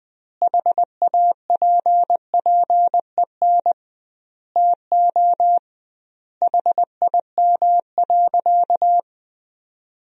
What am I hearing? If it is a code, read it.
Morse code, heard as a keyed tone, about 20 wpm, 700 Hz. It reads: HAPPEN TO HIM.
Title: 2x Word-Spacing - Sentences from Top 400 Words - 20wpm